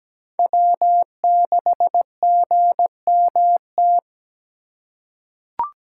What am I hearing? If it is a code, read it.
Morse code, heard as a keyed tone, about 17 wpm, 700 Hz. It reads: W6GMT
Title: USA Call Signs - 6x Character Spacing - 17wpm